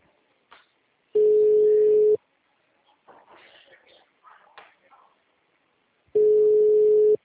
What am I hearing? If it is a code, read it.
Ringback tone (Europe, 425 Hz).